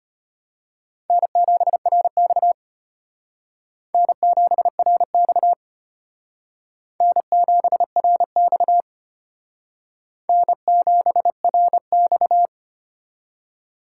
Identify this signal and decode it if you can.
Morse code, heard as a keyed tone, about 31 wpm, 700 Hz. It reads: N7RX N7RX N7RX N7RX
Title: Speed Racing - USA 1x2 Call Signs 25wpm